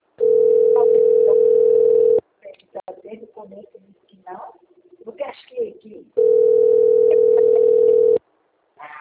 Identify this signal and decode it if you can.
Ringback tone (North America), 440 + 480 Hz, with speech.
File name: rg-1600-unknown- - - .53452